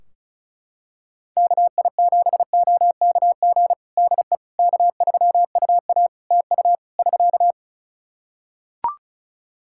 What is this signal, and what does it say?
Morse code, heard as a keyed tone, about 36 wpm, 700 Hz. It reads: KI7OKG DE K3UA TU <SK>
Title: Practice QSO - Exchange 4 - Ending - 35wpm